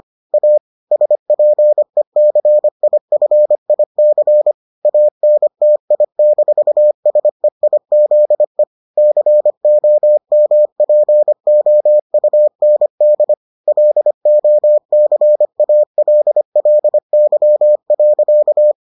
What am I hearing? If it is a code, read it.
Morse code, heard as a keyed tone, about 25 wpm, 600 Hz. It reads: A SPECIFIC ANTI-SEIZE COMPOUND LOCALLY.